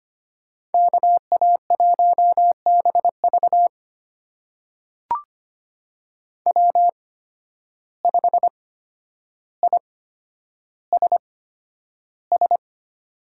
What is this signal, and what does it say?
Morse code, heard as a keyed tone, about 25 wpm, spaced as for 11 wpm, 700 Hz. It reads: KA1BV W5ISS
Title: USA Call Signs - 8x Character Spacing - 25wpm